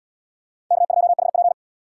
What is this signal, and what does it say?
Morse code, heard as a keyed tone, about 74 wpm, 700 Hz. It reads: K2HP